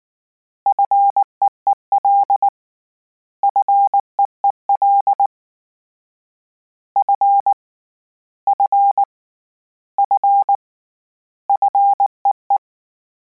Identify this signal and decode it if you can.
Morse code, heard as a keyed tone, about 19 wpm, 800 Hz. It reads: FEEL FEEL F F F FEE
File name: Combinations / Combinations-mp3s / Combinations@18